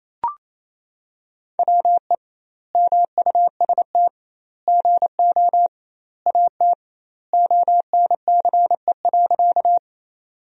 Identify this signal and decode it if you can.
Morse code, heard as a keyed tone, about 28 wpm, 700 Hz. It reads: WE MUST GO AT ONCE.
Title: 2x Word-Spacing - Sentences from Top 300 Words - 28wpm